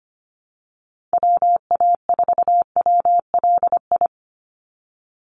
Morse code, heard as a keyed tone, about 25 wpm, 700 Hz. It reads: WA4WLI
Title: Call Signs Repeated 2x - 25wpm